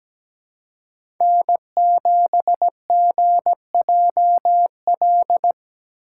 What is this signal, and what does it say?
Morse code, heard as a keyed tone, about 17 wpm, 700 Hz. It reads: N7GJL
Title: USA Call Signs - 3x Character Spacing - 17wpm